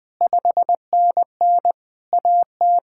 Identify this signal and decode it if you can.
Morse code, heard as a keyed tone, about 20 wpm, 700 Hz. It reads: 5NN AT
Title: POTA QSO Practice - 20wpm